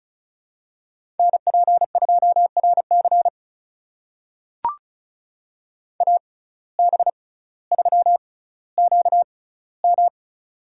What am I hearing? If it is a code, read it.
Morse code, heard as a keyed tone, about 36 wpm, spaced as for 18 wpm, 700 Hz. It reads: NP2RC AB3QM
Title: USA Call Signs - 5 Characters - 6x Character Spacing - 35wpm